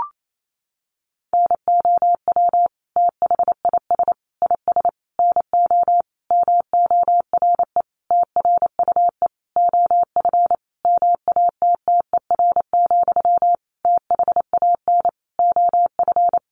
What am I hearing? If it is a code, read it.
Morse code, heard as a keyed tone, about 28 wpm, 700 Hz. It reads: NOW THIS IS NO MORE TRUE OF MATTER, THAN OF
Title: Sentences from Top 1400 Words 28wpm